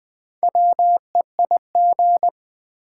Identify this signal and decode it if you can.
Morse code, heard as a keyed tone, about 20 wpm, 700 Hz. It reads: WEIG
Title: Build A Word - 1 Letter At A Time - 5-Letter Words - 20wpm